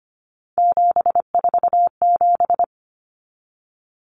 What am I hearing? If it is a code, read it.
Morse code, heard as a keyed tone, about 25 wpm, 700 Hz. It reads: 747
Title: Common Cut Numbers - 3-Digit 25wpm